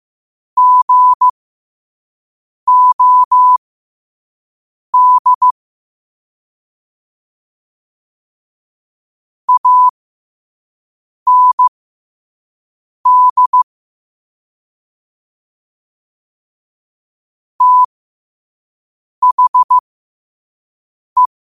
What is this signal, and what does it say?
Morse code, heard as a keyed tone, about 15 wpm, spaced as for 5 wpm, 1000 Hz. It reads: GOD AND THE
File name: qotdfiles / 5wpm